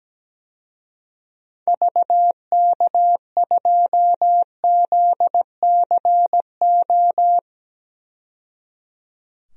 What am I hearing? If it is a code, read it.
Morse code, heard as a keyed tone, about 17 wpm, 700 Hz. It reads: VK2ZCO
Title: DXCC-100 Call Signs - 6 Characters - 17wpm